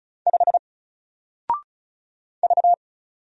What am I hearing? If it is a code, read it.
Morse code, heard as a keyed tone, about 36 wpm, 700 Hz. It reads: 5 V
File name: Single-Letter-Number-35wpm